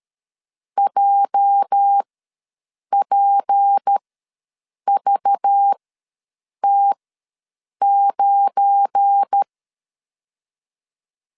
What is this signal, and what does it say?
Morse code, heard as a keyed tone, about 13 wpm, spaced as for 7 wpm, 800 Hz. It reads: JPVT9
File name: LEZIONI / 8 YJ45 / 8 YJ ws 12wpm 3342